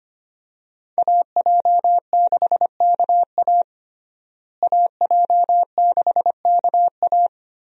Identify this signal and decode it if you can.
Morse code, heard as a keyed tone, about 25 wpm, 700 Hz. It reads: AJ6KA AJ6KA
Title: USA Call Signs - 5 Characters - Repeated 2x - 25wpm